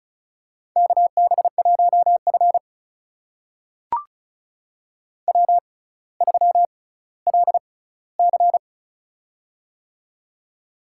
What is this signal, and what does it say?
Morse code, heard as a keyed tone, about 36 wpm, spaced as for 18 wpm, 700 Hz. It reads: KB1F W3LC
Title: USA Call Signs - 4 Characters - 6x Character Spacing - 35wpm